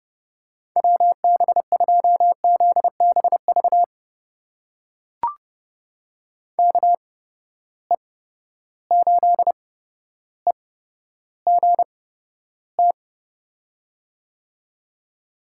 Morse code, heard as a keyed tone, about 30 wpm, spaced as for 13 wpm, 700 Hz. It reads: WB2ZBV KE8EGT
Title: USA Call Signs - 8x Character Spacing - 30wpm